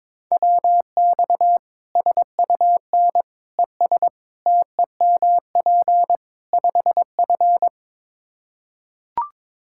Morse code, heard as a keyed tone, about 21 wpm, 700 Hz. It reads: WX SUN ES TEMP 5F
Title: Practice QSO - Exchange 2 - Info - 22wpm